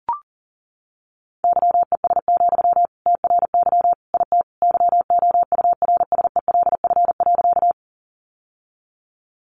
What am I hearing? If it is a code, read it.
Morse code, heard as a keyed tone, about 40 wpm, 700 Hz. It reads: YES, TRY IT YOURSELF.
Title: Sentences from Top 600 Words 40wpm